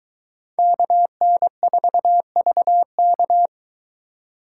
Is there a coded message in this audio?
Yes: Morse code from 0.6 to 3.5 s, heard as a keyed tone, about 22 wpm, 700 Hz. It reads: KN4VK